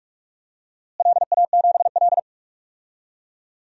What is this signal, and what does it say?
Morse code, heard as a keyed tone, about 42 wpm, 700 Hz. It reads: LA7L
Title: DXCC-100 Call Signs - 4 Characters - Repeated 3x - 45wpm